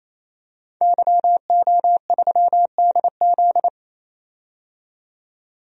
Morse code, heard as a keyed tone, about 28 wpm, 700 Hz. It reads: YO3DZ
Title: DXCC-100 Call Signs - 5 Characters - 28wpm